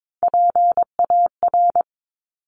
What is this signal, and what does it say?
Morse code, heard as a keyed tone, about 22 wpm, 700 Hz. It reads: PAR
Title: Top Three Letter Combinations in Words 22wpm